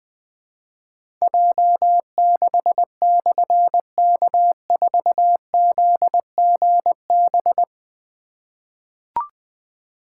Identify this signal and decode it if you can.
Morse code, heard as a keyed tone, about 20 wpm, 700 Hz. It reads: J6/K4ZGB
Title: Difficult Call Signs 20wpm